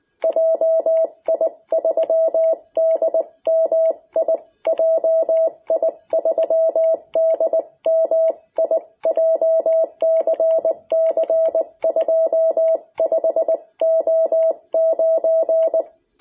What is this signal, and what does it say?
Morse code, heard as a keyed tone, about 20 wpm, 660 Hz. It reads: JI3DMI JI3DMI JCC25O9
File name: jh8cbh / 2021cw